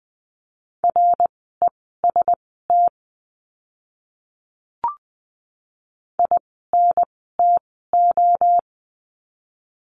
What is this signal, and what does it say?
Morse code, heard as a keyed tone, about 20 wpm, spaced as for 14 wpm, 700 Hz. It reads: REST INTO
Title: 054 Lesson 10f - Letter r - Words 20wpm